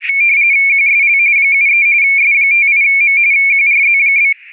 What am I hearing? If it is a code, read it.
RTTY, about 45.45 baud. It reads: QSL 73 S58M QRZ?